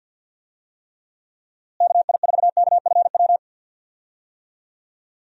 Morse code, heard as a keyed tone, about 50 wpm, 700 Hz. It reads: KI4KFW
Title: USA Call Signs - 6 Characters - Repeated 3x - 50wpm